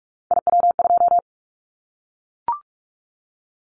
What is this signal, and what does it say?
Morse code, heard as a keyed tone, about 44 wpm, 700 Hz. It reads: IW2
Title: Callsign Prefixes for Top 25 DXCC - 45wpm